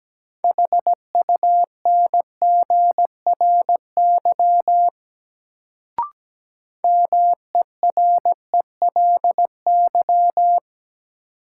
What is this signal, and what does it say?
Morse code, heard as a keyed tone, about 17 wpm, 700 Hz. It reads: HUNGRY MERELY
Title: Sentences from Top 1400 Words - New Word Drill - 17wpm